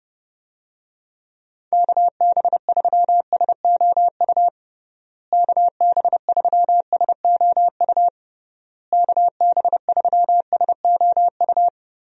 Morse code, heard as a keyed tone, about 30 wpm, 700 Hz. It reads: KB3SOU KB3SOU KB3SOU
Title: USA Call Signs - 6 Characters - Repeated 3x - 30wpm